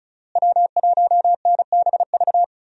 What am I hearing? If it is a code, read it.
Morse code, heard as a keyed tone, about 36 wpm, 700 Hz. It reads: W1NBV
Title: USA Call Signs - 5 Characters - Repeated 2x - 35wpm